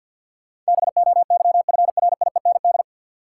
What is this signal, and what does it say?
Morse code, heard as a keyed tone, about 51 wpm, 700 Hz. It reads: BOYFRIEND
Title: Sentences from Top 1000 Words - Review Part 5 - New Word Drill - 50wpm